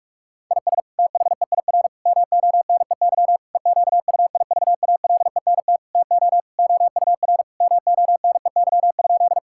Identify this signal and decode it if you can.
Morse code, heard as a keyed tone, about 46 wpm, 700 Hz. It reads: IS THEIR MONEY EQUIVALENT TO OUR MONEY?
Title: Sentences from Top 1600 Words 45wpm